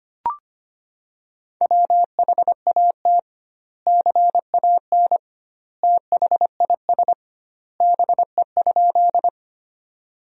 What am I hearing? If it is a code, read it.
Morse code, heard as a keyed tone, about 25 wpm, 700 Hz. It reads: WHAT CAN THIS BE?
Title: 2x Word-Spacing - Sentences from Top 100 Words - 25wpm